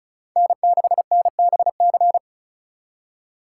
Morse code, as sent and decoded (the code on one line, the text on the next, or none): -. -.... -. -... -.-.
N6NBC